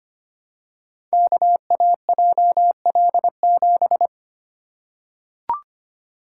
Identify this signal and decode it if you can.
Morse code, heard as a keyed tone, about 25 wpm, 700 Hz. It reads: KAJL7
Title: Code Groups - 5 Letter:Numbers - 25wpm